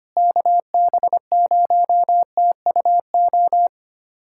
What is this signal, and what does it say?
Morse code, heard as a keyed tone, about 25 wpm, 700 Hz. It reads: KB0TUO
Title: USA Callsign DE Callsign 25wpm